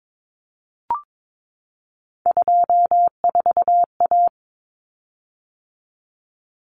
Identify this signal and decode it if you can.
Morse code, heard as a keyed tone, about 22 wpm, 700 Hz. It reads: 24A
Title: Common Cut Numbers - 3-Digit 22wpm